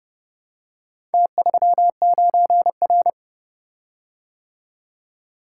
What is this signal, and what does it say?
Morse code, heard as a keyed tone, about 31 wpm, 700 Hz. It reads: T39R